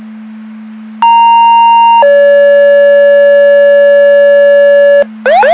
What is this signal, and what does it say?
Two-tone page, heard as a one-second tone, then a three-second tone, about 930 then 570 Hz. It is heard from 1.0 to 5.0 s.